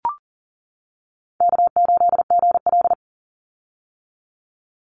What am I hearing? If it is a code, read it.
Morse code, heard as a keyed tone, about 40 wpm, 700 Hz. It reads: K8GL
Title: Easy Call Signs - 40wpm